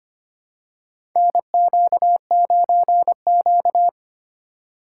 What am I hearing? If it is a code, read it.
Morse code, heard as a keyed tone, about 25 wpm, 700 Hz. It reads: NQ9Q
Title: USA Call Signs - 4 Characters - 2x Character Spacing - 25wpm